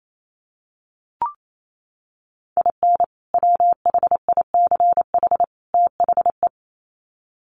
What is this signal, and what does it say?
Morse code, heard as a keyed tone, about 28 wpm, 700 Hz. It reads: IN WHICH THE
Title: Sets of 3 Words 28wpm